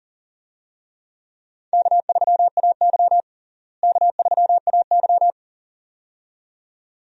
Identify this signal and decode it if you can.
Morse code, heard as a keyed tone, about 40 wpm, 700 Hz. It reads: K3AY K3AY
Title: USA Call Signs - 4 Characters - Repeated 2x - 40wpm